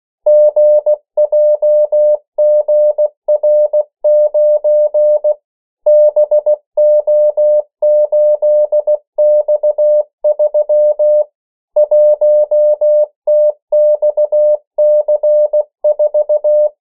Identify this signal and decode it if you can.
Morse code, heard as a keyed tone, about 16 wpm, 600 Hz. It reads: GJGR9 BO8X3 1TXC4
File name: morse mp3 / 14 urok 13